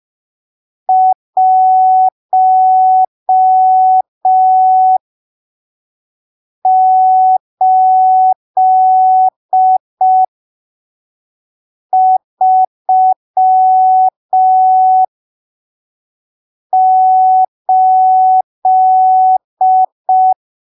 Morse code, heard as a keyed tone, about 5 wpm, spaced as for 3 wpm, 750 Hz. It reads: 1838
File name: arquivos / texto / texto 17 05ppm